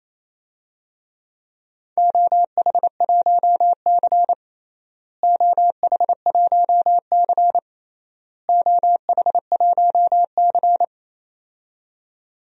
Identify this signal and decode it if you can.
Morse code, heard as a keyed tone, about 28 wpm, 700 Hz. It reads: OH1C OH1C OH1C
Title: DXCC-100 Call Signs - 4 Characters - Repeated 3x - 28wpm